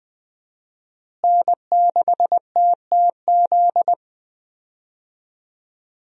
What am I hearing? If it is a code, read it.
Morse code, heard as a keyed tone, about 20 wpm, 700 Hz. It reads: N6TTZ
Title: USA Call Signs - 5 Characters - Repeated 3x - 20wpm